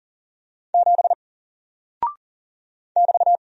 Morse code, heard as a keyed tone, about 41 wpm, 700 Hz. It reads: 7 <BT>
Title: Single Character (Letter:Number:Punctuation:Prosign) 40wpm